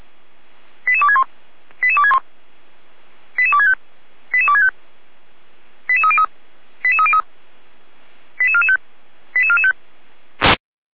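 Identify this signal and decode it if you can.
Five-tone selective call (ZVEI), heard as quick runs of five tones, tones of about 70 ms. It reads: 88261 88261 88265 88265 88303 88303 88445 88445
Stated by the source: ZVEI five-tone selective call received by SDR, archived by Feuerwehr Orlamünde (fire brigade).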